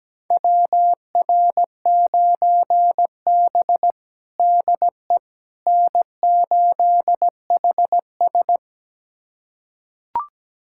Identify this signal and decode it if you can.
Morse code, heard as a keyed tone, about 17 wpm, 700 Hz. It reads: WR9B DE N8HS
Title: USA Callsign DE Callsign - 4 Characters - 17wpm